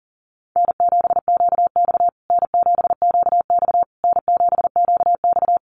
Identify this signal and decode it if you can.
Morse code, heard as a keyed tone, about 40 wpm, 700 Hz. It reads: N7QX N7QX N7QX
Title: USA Call Signs Repeated 3x 40wpm